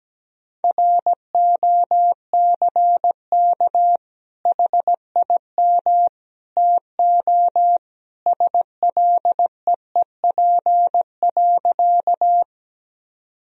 Morse code, heard as a keyed tone, about 17 wpm, 700 Hz. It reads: ROCK HIM TO SLEEP.
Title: Sentences from Top 1000 Words - Review Part 2 - 17wpm